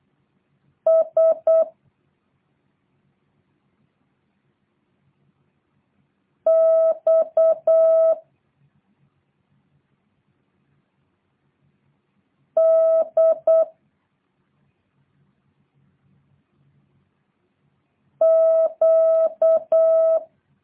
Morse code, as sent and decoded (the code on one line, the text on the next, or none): ... / -..- / -.. / --.-
S X D Q